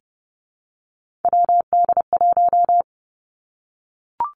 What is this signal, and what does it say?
Morse code, heard as a keyed tone, about 30 wpm, 700 Hz. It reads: WD1
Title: Three Challenging Letter-Number Combinations 30wpm